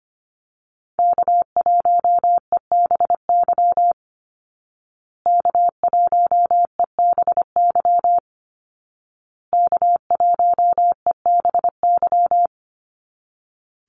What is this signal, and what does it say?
Morse code, heard as a keyed tone, about 25 wpm, 700 Hz. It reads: K1EBY K1EBY K1EBY
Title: Call Signs Repeated 2x - 25wpm